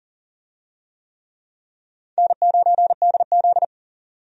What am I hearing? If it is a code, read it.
Morse code, heard as a keyed tone, about 40 wpm, 700 Hz. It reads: N9DZ
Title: USA Call Signs - 4 Characters - 40wpm